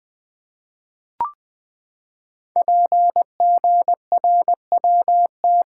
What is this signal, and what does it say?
Morse code, heard as a keyed tone, about 20 wpm, 700 Hz. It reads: PGRWT
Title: Code Groups - 5 Letters - 20wpm